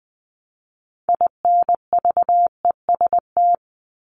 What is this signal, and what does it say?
Morse code, heard as a keyed tone, about 20 wpm, 700 Hz. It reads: INVEST
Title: Top 1001-1250 Words 20wpm